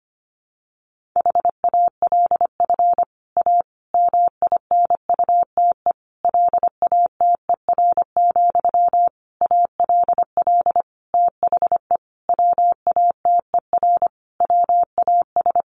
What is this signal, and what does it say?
Morse code, heard as a keyed tone, about 25 wpm, 700 Hz. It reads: HALF A MINUTE LATER, ALL THE WATER WAS
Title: Sentences from Top 1100 Words 25wpm